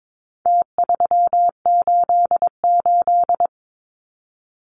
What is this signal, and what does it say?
Morse code, heard as a keyed tone, about 22 wpm, 700 Hz. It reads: T388
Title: Numbers - Contest Style - 4-Digit 22wpm